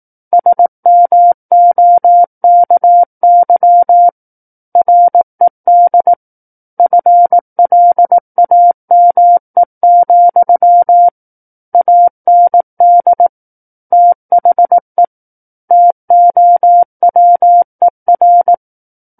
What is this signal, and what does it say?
Morse code, heard as a keyed tone, about 18 wpm, 700 Hz. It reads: SMOKY RED FLAME, AND THE TOWER
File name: Book One / 09-Chapter 9 - 18 WPM